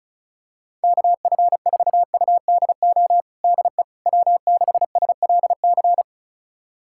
Morse code, heard as a keyed tone, about 36 wpm, 700 Hz. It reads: KF4UDO DE W6SLC
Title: USA Callsign DE Callsign 35wpm